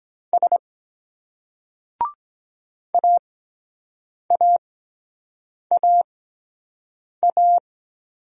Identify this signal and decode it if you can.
Morse code, heard as a keyed tone, about 23 wpm, spaced as for 7 wpm, 700 Hz. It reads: S AAAA